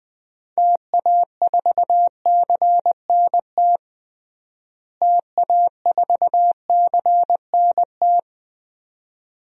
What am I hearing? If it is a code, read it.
Morse code, heard as a keyed tone, about 20 wpm, 700 Hz. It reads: TA4CNT TA4CNT
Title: DXCC-100 Call Signs - 6 Characters - Repeated 2x - 20wpm